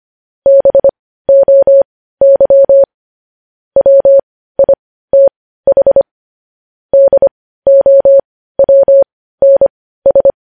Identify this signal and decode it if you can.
Morse code, heard as a keyed tone, about 25 wpm, spaced as for 15 wpm, 550 Hz. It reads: BOY WITH DOWNS
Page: news headlines in morse code at 15 wpm